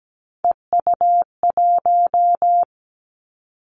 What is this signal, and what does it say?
Morse code, heard as a keyed tone, about 17 wpm, 700 Hz. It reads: EU1
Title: Callsign Prefixes for Top 100 DXCC - 17wpm